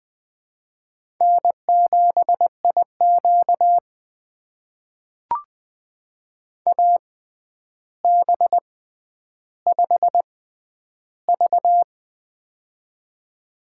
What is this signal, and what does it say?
Morse code, heard as a keyed tone, about 20 wpm, spaced as for 10 wpm, 700 Hz. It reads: N7IQ AB5V
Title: USA Call Signs - 4 Characters - 6x Character Spacing - 20wpm